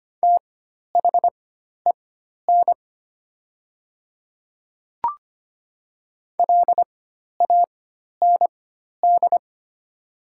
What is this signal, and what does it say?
Morse code, heard as a keyed tone, about 25 wpm, spaced as for 12 wpm, 700 Hz. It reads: THEN LAND
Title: 4x Character Spacing - Top 300 Words - 25wpm